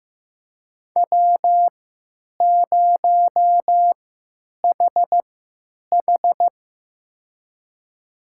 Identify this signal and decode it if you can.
Morse code, heard as a keyed tone, about 15 wpm, spaced as for 8 wpm, 700 Hz. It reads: W0HH
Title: USA Call Signs - 4 Characters - 3x Character Spacing - 15wpm